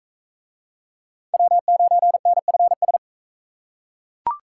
Morse code, heard as a keyed tone, about 39 wpm, 700 Hz. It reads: W9NFS